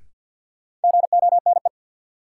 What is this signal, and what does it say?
Morse code, heard as a keyed tone, about 53 wpm, 700 Hz. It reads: GONE
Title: Sentences from Top 1200 Words - New Word Drill - 50wpm